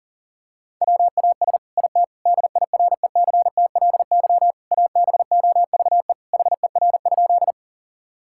Morse code, heard as a keyed tone, about 37 wpm, 700 Hz. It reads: WAS IT DIRECTLY ABOVE HER?
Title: Sentences from Top 1000 Words - Review Part 5 - 40wpm